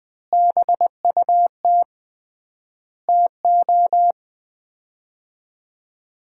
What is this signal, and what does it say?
Morse code, heard as a keyed tone, about 20 wpm, 700 Hz. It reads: BUT TO
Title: 3x Word-Spacing - Sets of 5 Words - 20wpm